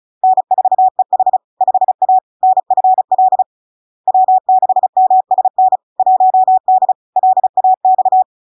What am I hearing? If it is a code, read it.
Morse code, heard as a keyed tone, about 36 wpm, 750 Hz. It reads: N4EH 5A NFL W6MSN 1D LAX
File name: LARK-FD-2022-Log-in-CW-35wpm-750-Hz